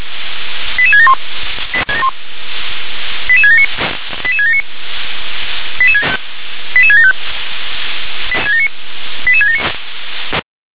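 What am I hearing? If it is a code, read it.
Five-tone selective call (ZVEI), heard as quick runs of five tones, tones of about 70 ms. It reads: 88671 88679 88679 88675 88679
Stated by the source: ZVEI five-tone selective call, code 88771, received by SDR.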